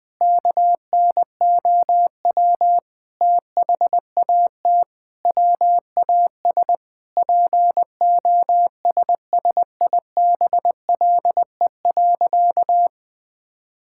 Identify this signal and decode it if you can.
Morse code, heard as a keyed tone, about 20 wpm, 700 Hz. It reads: KNOW THAT WAS POSSIBLE.
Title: Sentences with 5-Word Phrases - Spoken Beginning - 20wpm